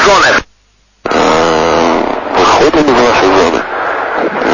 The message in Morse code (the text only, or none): none